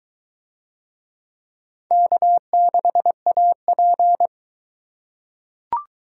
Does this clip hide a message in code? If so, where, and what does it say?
Yes: Morse code from 1.9 to 4.3 s, heard as a keyed tone, about 22 wpm, 700 Hz. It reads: K6AP